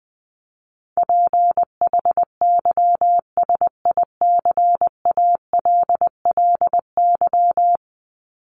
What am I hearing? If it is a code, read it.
Morse code, heard as a keyed tone, about 20 wpm, 700 Hz. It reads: PHYSICALLY
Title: Top 1501-1750 Words 20wpm